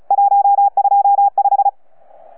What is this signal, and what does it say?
Morse code, heard as a keyed tone, about 36 wpm, 800 Hz. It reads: 125
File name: UP2L-20-CW